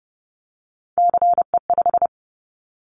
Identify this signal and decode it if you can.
Morse code, heard as a keyed tone, about 30 wpm, 700 Hz. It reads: CE5
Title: Callsign Prefixes for Top 100 DXCC - 30wpm